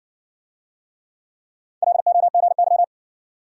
Morse code, heard as a keyed tone, about 57 wpm, 700 Hz. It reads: LOGY